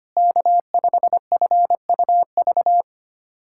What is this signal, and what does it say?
Morse code, heard as a keyed tone, about 25 wpm, 700 Hz. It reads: K5FUV
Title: K1USN SST QSO Practice - 25wpm